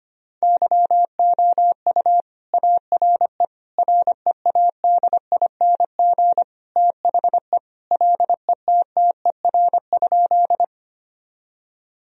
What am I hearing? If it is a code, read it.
Morse code, heard as a keyed tone, about 25 wpm, 700 Hz. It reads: YOU ARE READING THE LETTER?
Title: Sentences from Top 1000 Words - Review Part 3 - 25wpm